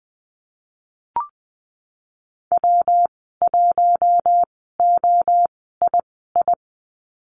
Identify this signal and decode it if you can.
Morse code, heard as a keyed tone, about 20 wpm, spaced as for 15 wpm, 700 Hz. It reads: W1OII
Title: 140 Lesson 23g - Letter f - Call Signs 20wpm